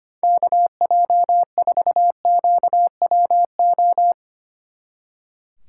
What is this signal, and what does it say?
Morse code, heard as a keyed tone, about 32 wpm, 700 Hz. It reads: KJ4QWO